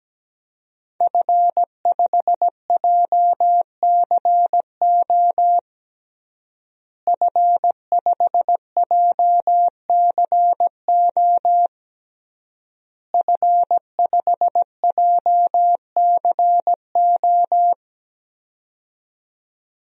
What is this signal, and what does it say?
Morse code, heard as a keyed tone, about 17 wpm, 700 Hz. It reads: F5JCO F5JCO F5JCO
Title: DXCC-100 Call Signs - 5 Characters - Repeated 3x - 17wpm